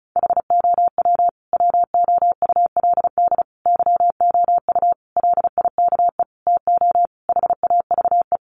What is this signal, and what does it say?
Morse code, heard as a keyed tone, about 34 wpm, 700 Hz. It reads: HOW WOULD YOU LIKE TO HAVE
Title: Sentences from Top 100 Words 35wpm